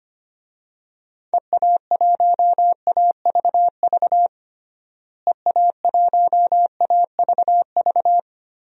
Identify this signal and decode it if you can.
Morse code, heard as a keyed tone, about 25 wpm, 700 Hz. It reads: EA1AVV EA1AVV
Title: DXCC-100 Call Signs - 6 Characters - Repeated 3x - 25wpm